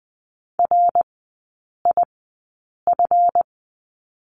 Morse code, heard as a keyed tone, about 20 wpm, spaced as for 8 wpm, 700 Hz. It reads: RIF
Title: 138 Lesson 23e - Letter f - Three Characters 20wpm